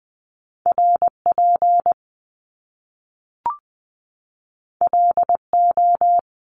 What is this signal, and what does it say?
Morse code, heard as a keyed tone, about 20 wpm, 700 Hz. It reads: R P L O
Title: Top Two Letter Combinations in Words 20wpm